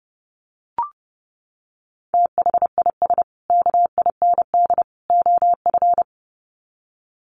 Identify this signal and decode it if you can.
Morse code, heard as a keyed tone, about 30 wpm, 700 Hz. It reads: THIS KIND OF
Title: Sets of 3 Words 30wpm